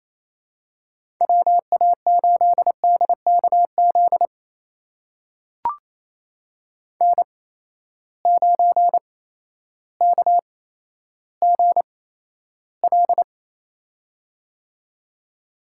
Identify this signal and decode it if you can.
Morse code, heard as a keyed tone, about 28 wpm, spaced as for 13 wpm, 700 Hz. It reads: WA8DKZ N9KGL
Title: USA Call Signs - 8x Character Spacing - 28wpm